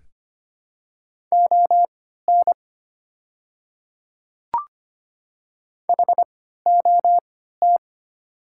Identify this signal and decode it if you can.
Morse code, heard as a keyed tone, about 25 wpm, spaced as for 14 wpm, 700 Hz. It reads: ON HOT
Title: 3x Character Spacing - Top 100 Words - 25wpm